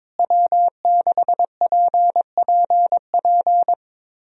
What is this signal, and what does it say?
Morse code, heard as a keyed tone, about 22 wpm, 700 Hz. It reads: W6PPP
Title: USA Call Signs - 5 Characters - Repeated 2x - 22wpm